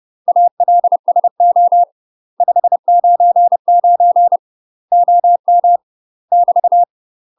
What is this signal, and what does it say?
Morse code, heard as a keyed tone, about 30 wpm, 700 Hz. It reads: ALSO 599 OM <BT>
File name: UK MASTER QSOs TRAINING / 30WPM / UK MASTER QSOs 0005